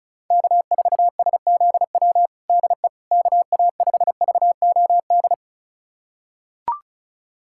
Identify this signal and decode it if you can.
Morse code, heard as a keyed tone, about 34 wpm, 700 Hz. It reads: K4SZW DE KA5VOD
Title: USA Callsign DE Callsign 35wpm